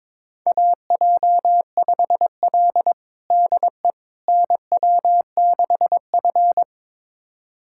Morse code, heard as a keyed tone, about 22 wpm, 700 Hz. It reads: AJ5L DE NW6F
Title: USA Callsign DE Callsign - 4 Characters - 22wpm